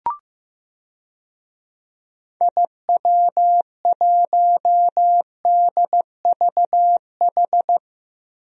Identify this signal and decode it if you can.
Morse code, heard as a keyed tone, about 16 wpm, 700 Hz. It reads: IW1DVH
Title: DXCC-100 Call Signs - 6 Characters - 15wpm